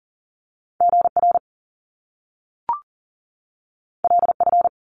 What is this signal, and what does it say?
Morse code, heard as a keyed tone, about 40 wpm, 700 Hz. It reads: GR LF
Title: Top Two Letter Combinations in Words 40wpm